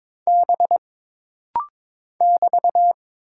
Morse code, heard as a keyed tone, about 22 wpm, 700 Hz. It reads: B <BT>
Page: This rapid-fire timing is geared to push your ICR (Instant Character Recognition) skills.